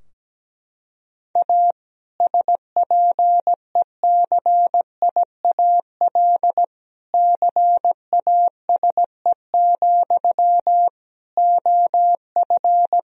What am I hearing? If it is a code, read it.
Morse code, heard as a keyed tone, about 17 wpm, 700 Hz. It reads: A SPECIAL CASE, OF
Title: Sentences from Top 1000 Words - Review Part 3 - 17wpm